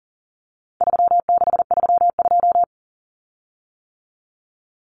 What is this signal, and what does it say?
Morse code, heard as a keyed tone, about 40 wpm, 700 Hz. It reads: 3632
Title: Common Cut Numbers - 4-Digit 40wpm